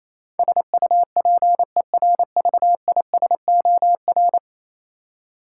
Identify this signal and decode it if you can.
Morse code, heard as a keyed tone, about 28 wpm, 700 Hz. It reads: SUPERVISOR
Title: Practice QSO - Exchange 3 - More Info - 28wpm